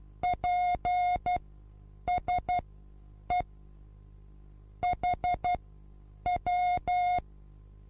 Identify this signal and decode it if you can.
Morse code, heard as a keyed tone, about 12 wpm, spaced as for 8 wpm, 720 Hz. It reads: PSE HW
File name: repos / Morsekurs / DJ4UF Morsen23